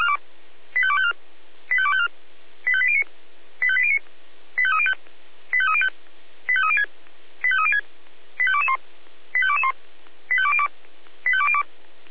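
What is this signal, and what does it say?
Five-tone selective call (ZVEI), heard as quick runs of five tones, tones of about 70 ms. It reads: 85254 85254 85809 85809 85305 85305 85306 85306 85221 85221 85202 85202
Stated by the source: ZVEI five-tone selective call received by SDR.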